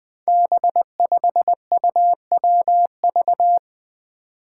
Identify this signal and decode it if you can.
Morse code, heard as a keyed tone, about 20 wpm, 700 Hz. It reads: B5UWV
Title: Five Challenging Letter-Number Combinations 20wpm